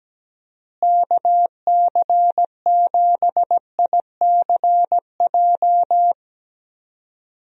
Morse code, heard as a keyed tone, about 17 wpm, 700 Hz. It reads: KC7ICJ
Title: USA Call Signs - 6 Characters - Repeated 3x - 17wpm